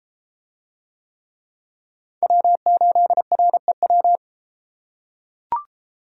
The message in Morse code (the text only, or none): W8REW